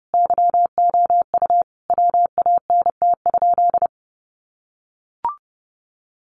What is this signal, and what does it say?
Morse code, heard as a keyed tone, about 30 wpm, 700 Hz. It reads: YOU WANT?
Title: Sentences from Top 200 Words 30wpm